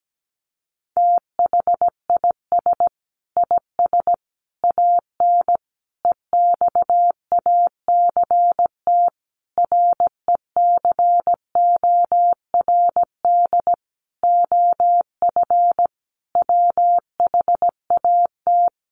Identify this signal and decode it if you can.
Morse code, heard as a keyed tone, about 17 wpm, 700 Hz. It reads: THIS IS AN EXACT RECORD OF WHAT
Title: Sentences from Top 1300 Words 17wpm